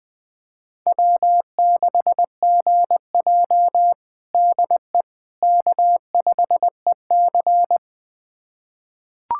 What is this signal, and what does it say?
Morse code, heard as a keyed tone, about 20 wpm, 700 Hz. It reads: W6GJ DE K5EC
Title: USA Callsign DE Callsign - 4 Characters - 20wpm